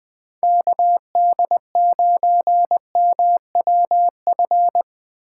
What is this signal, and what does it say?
Morse code, heard as a keyed tone, about 21 wpm, 700 Hz. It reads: KD9MWF
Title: Speed Racing - USA 2x3 Call Signs 17wpm